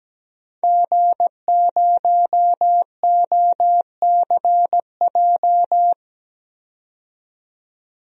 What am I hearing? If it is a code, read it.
Morse code, heard as a keyed tone, about 17 wpm, 700 Hz. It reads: G0OCJ
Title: DXCC-100 Call Signs - 5 Characters - 17wpm